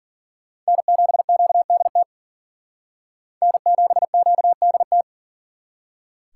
Morse code, heard as a keyed tone, about 53 wpm, 700 Hz. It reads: N7QDT N7QDT